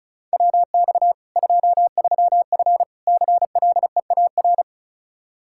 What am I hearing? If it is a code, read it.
Morse code, heard as a keyed tone, about 36 wpm, 700 Hz. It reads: WX 23F CLEAR
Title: K1USN SST QSO Practice 35wpm